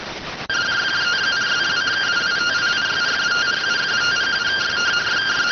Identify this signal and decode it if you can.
Two-tone teleprinter signal, about 170 Hz shift.